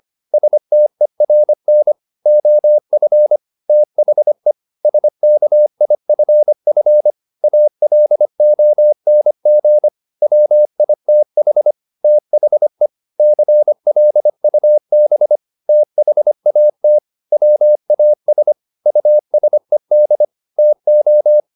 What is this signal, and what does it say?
Morse code, heard as a keyed tone, about 25 wpm, 600 Hz. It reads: STERN OF THE SKIFF ALONG WITH THE CLUB THAT WAS USED TO
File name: the old man and the sea part 1 25wpm